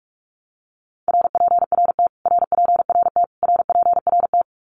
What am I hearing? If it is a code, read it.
Morse code, heard as a keyed tone, about 44 wpm, 700 Hz. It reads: RPRT RPRT RPRT
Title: Abbreviations Repeated 3x - 45wpm